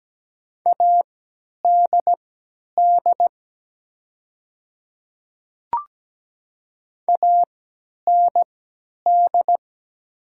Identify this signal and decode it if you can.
Morse code, heard as a keyed tone, about 17 wpm, spaced as for 10 wpm, 700 Hz. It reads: ADD AND
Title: 3x Character Spacing - Top 200 Words - 17wpm